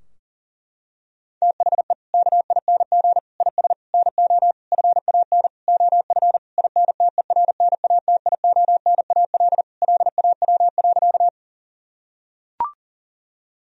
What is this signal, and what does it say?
Morse code, heard as a keyed tone, about 40 wpm, 700 Hz. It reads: THE KING IS NO FAN OF INTERNATIONAL LAW.
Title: Sentences from Top 1000 Words - Review Part 4 - 40wpm